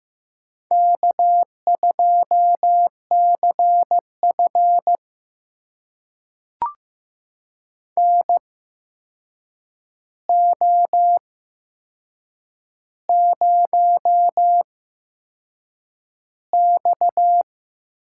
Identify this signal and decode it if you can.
Morse code, heard as a keyed tone, about 15 wpm, spaced as for 6 wpm, 700 Hz. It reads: K2CF NO0X
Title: USA Call Signs - 8x Character Spacing - 15wpm